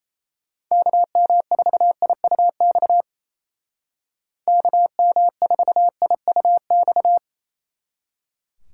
Morse code, heard as a keyed tone, about 33 wpm, 700 Hz. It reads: KM4IUX KM4IUX
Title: Speed Racing - USA 2x3 Call Signs 28wpm